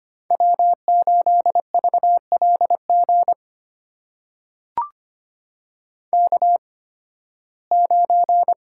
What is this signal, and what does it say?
Morse code, heard as a keyed tone, about 25 wpm, spaced as for 16 wpm, 700 Hz. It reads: W8VLG K9
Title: USA Call Signs - 5 Characters - 8x Character Spacing - 25wpm